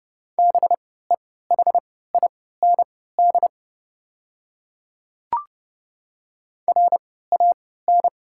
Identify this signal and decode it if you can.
Morse code, heard as a keyed tone, about 32 wpm, spaced as for 17 wpm, 690 Hz. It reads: BEHIND RAN
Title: 3x Character Spacing - Top 500 Words - 30wpm